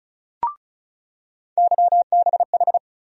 Morse code, heard as a keyed tone, about 36 wpm, 700 Hz. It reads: YBH
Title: USA Call Sign Postfixes - 3 Letters 35wpm